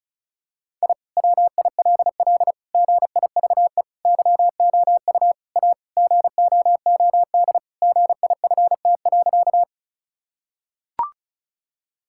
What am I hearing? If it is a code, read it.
Morse code, heard as a keyed tone, about 36 wpm, 700 Hz. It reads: I WILL GIVE YOU A GOOD GIFT.
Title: Sentences from Top 1000 Words - Review Part 6 - 35wpm